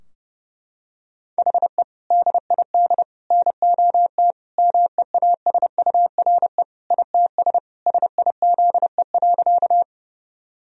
Morse code, heard as a keyed tone, about 30 wpm, 700 Hz. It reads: HE DID NOT MEASURE ITS SIZE.
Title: Sentences from Top 1000 Words - Review Part 2 - 30wpm